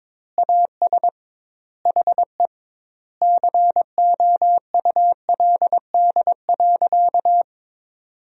Morse code, heard as a keyed tone, about 22 wpm, 700 Hz. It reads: AS HE COULD.
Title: 2x Word-Spacing - Sentences from Top 500 Words - 22wpm